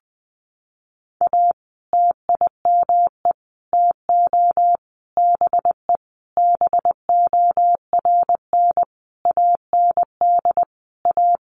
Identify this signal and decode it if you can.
Morse code, heard as a keyed tone, about 20 wpm, 700 Hz. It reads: A TIME TO BE BORN AND A
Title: Sentences from Top 1300 Words 20wpm